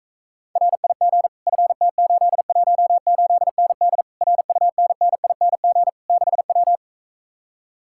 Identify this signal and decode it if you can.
Morse code, heard as a keyed tone, about 45 wpm, 700 Hz. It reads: RIG FT818ND RUNNING 6W